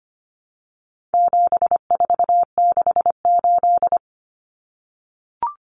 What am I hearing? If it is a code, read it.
Morse code, heard as a keyed tone, about 25 wpm, 700 Hz. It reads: 7468
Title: Numbers - 4-Digit 25wpm